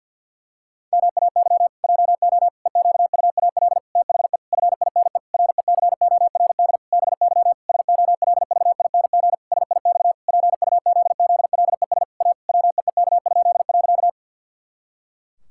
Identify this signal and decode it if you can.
Morse code, heard as a keyed tone, about 50 wpm, 700 Hz. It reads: MAY JO EQUAL THE FINE RECORD BY SOLVING SIX PUZZLES A WEEK?.
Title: Pangram Challenge v2 - 50wpm